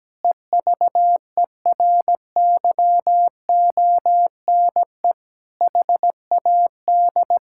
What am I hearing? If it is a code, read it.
Morse code, heard as a keyed tone, about 17 wpm, 700 Hz. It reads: EVERYONE HAD
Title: Sentences from Top 1000 Words - Review Part 5 - 17wpm